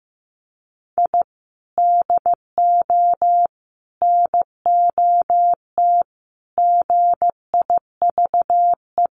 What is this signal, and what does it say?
Morse code, heard as a keyed tone, about 15 wpm, 700 Hz. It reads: I DO NOT GIVE
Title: Sets of 5 Words Encore2 15wpm